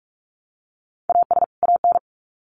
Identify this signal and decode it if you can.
Morse code, heard as a keyed tone, about 44 wpm, 700 Hz. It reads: AS AN
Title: Sets of 2 Words 45wpm